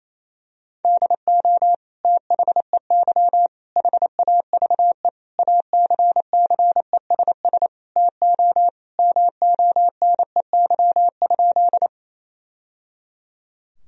Morse code, heard as a keyed tone, about 28 wpm, 700 Hz. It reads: DO THEY HAVE ACCESS TO MONEY?
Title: Sentences from Top 1000 Words - Review Part 4 - 28wpm